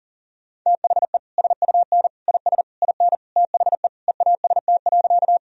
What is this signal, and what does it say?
Morse code, heard as a keyed tone, about 40 wpm, 700 Hz. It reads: THE SUN IS IN THE EAST.
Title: Sentences from Top 1000 Words - Review Part 3 - 40wpm